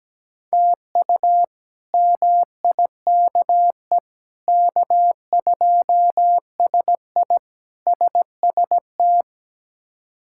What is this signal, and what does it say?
Morse code, heard as a keyed tone, about 17 wpm, 700 Hz. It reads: TU MIKE K2SI SST
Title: K1USN SST QSO Practice - 17wpm